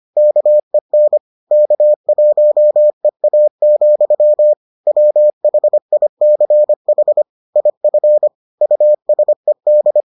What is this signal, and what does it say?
Morse code, heard as a keyed tone, about 25 wpm, 600 Hz. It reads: KEN K1EA, WHICH IF USED